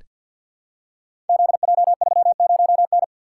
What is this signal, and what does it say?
Morse code, heard as a keyed tone, about 46 wpm, 700 Hz. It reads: ZJ30N
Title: Code Groups - 5 Letter:Numbers - 50wpm